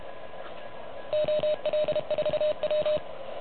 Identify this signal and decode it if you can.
Morse code, heard as a keyed tone, about 30 wpm, 600 Hz. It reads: OL4W
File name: truSDX / MWC29.8